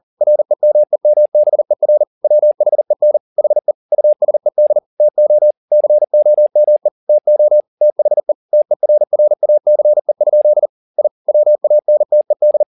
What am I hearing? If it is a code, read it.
Morse code, heard as a keyed tone, about 40 wpm, 600 Hz. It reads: REMEMBER WHEN HE USED TO COME TO THE TERRACE? I WANTED